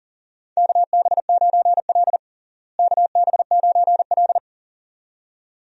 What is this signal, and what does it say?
Morse code, heard as a keyed tone, about 40 wpm, 700 Hz. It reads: KB9L KB9L
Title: USA Call Signs - 4 Characters - Repeated 2x - 40wpm